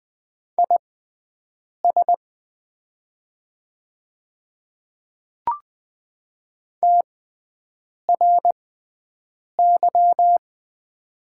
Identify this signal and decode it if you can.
Morse code, heard as a keyed tone, about 20 wpm, spaced as for 7 wpm, 700 Hz. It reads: IS TRY
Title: 6x Character Spacing - Top 200 Words - 20wpm